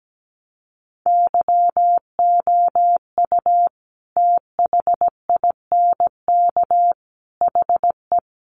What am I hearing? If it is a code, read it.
Morse code, heard as a keyed tone, about 17 wpm, 700 Hz. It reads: YOU THINK HE
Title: Sentences from Top 800 Words 17wpm